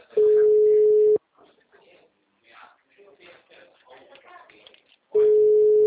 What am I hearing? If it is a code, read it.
Ringback tone (Europe, 425 Hz), with speech.